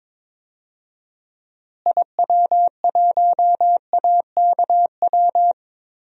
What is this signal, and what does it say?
Morse code, heard as a keyed tone, about 22 wpm, 700 Hz. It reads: IW1AKW
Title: DXCC-100 Call Signs - 6 Characters - Repeated 3x - 22wpm